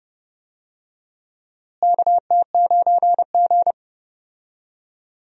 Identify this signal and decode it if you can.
Morse code, heard as a keyed tone, about 30 wpm, 700 Hz. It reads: KT9G
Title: USA Call Signs - 4 Characters - Repeated 2x - 30wpm